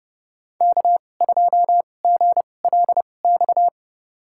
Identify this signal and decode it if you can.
Morse code, heard as a keyed tone, about 30 wpm, spaced as for 22 wpm, 700 Hz. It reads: K2GLX
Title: USA Call Signs - 2x Character Spacing - 30wpm